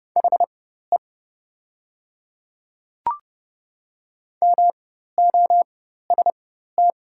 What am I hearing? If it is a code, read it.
Morse code, heard as a keyed tone, about 30 wpm, spaced as for 14 wpm, 700 Hz. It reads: HE MOST
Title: 4x Character Spacing - Top 100 Words - 30wpm